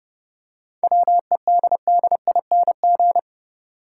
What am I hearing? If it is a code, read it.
Morse code, heard as a keyed tone, about 30 wpm, 700 Hz. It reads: WEDDING
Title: Seven Letter Words - 30wpm